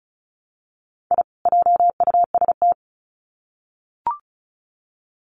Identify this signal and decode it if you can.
Morse code, heard as a keyed tone, about 34 wpm, 700 Hz. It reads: I JUST
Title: Sets of 2 Words 35wpm